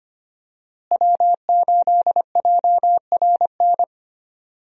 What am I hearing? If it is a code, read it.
Morse code, heard as a keyed tone, about 25 wpm, 700 Hz. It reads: W8JRN
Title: USA Call Signs - 5 Characters - Repeated 3x - 25wpm